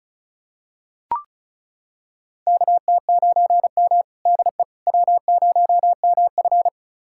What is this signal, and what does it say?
Morse code, heard as a keyed tone, about 36 wpm, 700 Hz. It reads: KT9M DE W0MF
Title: USA Callsign DE Callsign - 4 Characters - 35wpm